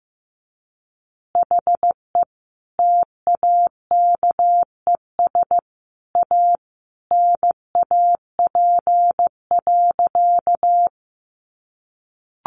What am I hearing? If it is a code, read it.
Morse code, heard as a keyed tone, about 15 wpm, 700 Hz. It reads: HE TAKES A NAP.
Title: Easy Phrases 15wpm